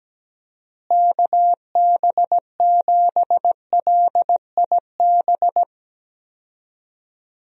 Morse code, heard as a keyed tone, about 17 wpm, 700 Hz. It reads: KB7LIB
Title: USA Call Signs - 6 Characters - Repeated 2x - 17wpm